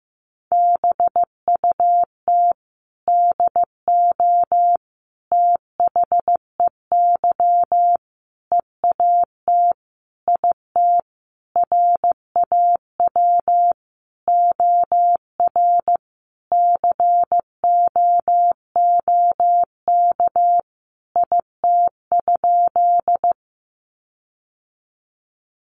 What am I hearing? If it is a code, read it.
Morse code, heard as a keyed tone, about 15 wpm, 700 Hz. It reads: BUT DO THEY EAT IT RAW OR COOK IT?
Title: Sentences from Top 1500 Words 15wpm